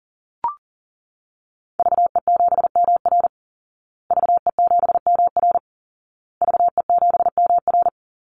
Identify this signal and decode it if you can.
Morse code, heard as a keyed tone, about 40 wpm, 700 Hz. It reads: VE7MR VE7MR VE7MR
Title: Call Signs Repeated 2x - 40wpm